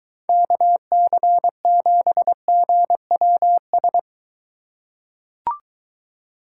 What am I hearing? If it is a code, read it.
Morse code, heard as a keyed tone, about 23 wpm, 700 Hz. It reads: KC7GWS